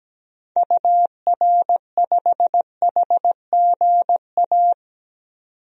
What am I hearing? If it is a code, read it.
Morse code, heard as a keyed tone, about 17 wpm, 700 Hz. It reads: UR5HGA
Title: DXCC-100 Call Signs - 6 Characters - 17wpm